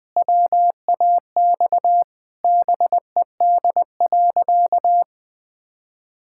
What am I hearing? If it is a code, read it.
Morse code, heard as a keyed tone, about 20 wpm, 700 Hz. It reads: WAX BED.
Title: Pangram Challenge v2 - 20wpm